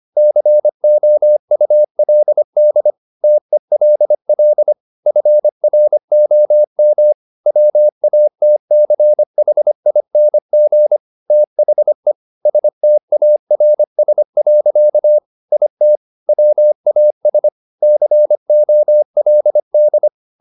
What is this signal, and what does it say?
Morse code, heard as a keyed tone, about 25 wpm, 600 Hz. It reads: COULD TELL FROM WATCHING THE STARS. IT WAS COLD
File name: the old man and the sea part 1 25wpm